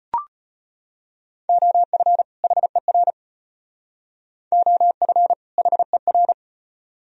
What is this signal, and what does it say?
Morse code, heard as a keyed tone, about 32 wpm, 700 Hz. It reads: OF HER OF HER